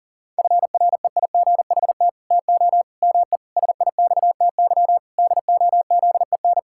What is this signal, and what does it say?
Morse code, heard as a keyed tone, about 40 wpm, 700 Hz. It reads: FREIGHT TO ME SIXTY DOZEN
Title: Pangram Challenge v2 - 40wpm